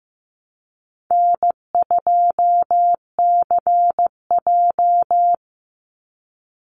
Morse code, heard as a keyed tone, about 15 wpm, 700 Hz. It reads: N2CJ
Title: Call Signs Repeated 3x - 15wpm